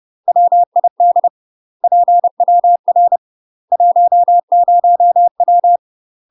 Morse code, heard as a keyed tone, about 30 wpm, 700 Hz. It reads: WID PWR 10W
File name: UK MASTER QSOs TRAINING / 30WPM / UK MASTER QSOs 0013